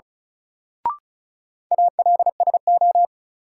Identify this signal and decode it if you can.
Morse code, heard as a keyed tone, about 36 wpm, 700 Hz. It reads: ALSO
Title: Sentences from Top 400 Words - New Word Drill - 35wpm